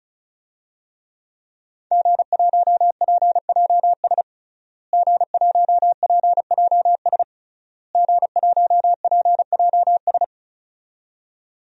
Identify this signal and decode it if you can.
Morse code, heard as a keyed tone, about 36 wpm, 700 Hz. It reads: G1PJS G1PJS G1PJS
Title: DXCC-100 Call Signs - 5 Characters - Repeated 3x - 35wpm